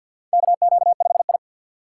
Morse code, heard as a keyed tone, about 46 wpm, 700 Hz. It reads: KQ5I
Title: USA Call Signs - 3x Character Spacing - 50wpm